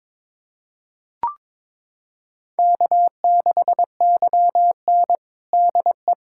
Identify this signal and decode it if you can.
Morse code, heard as a keyed tone, about 22 wpm, 700 Hz. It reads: K6YN DE
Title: USA Callsign DE Callsign - 4 Characters - 22wpm